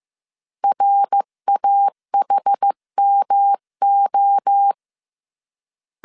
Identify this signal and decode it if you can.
Morse code, heard as a keyed tone, about 15 wpm, 800 Hz. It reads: RAHMO
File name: CW / DET / 1 3 1500C S C 14wpm 2458